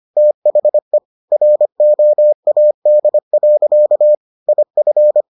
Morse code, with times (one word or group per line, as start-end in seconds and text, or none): 0.2-1.0 THE
1.3-4.2 ROAD.
4.5-5.2 IF